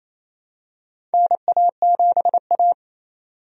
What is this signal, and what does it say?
Morse code, heard as a keyed tone, about 27 wpm, 700 Hz. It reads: NA7A
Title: USA Call Signs - 4 Characters - 28wpm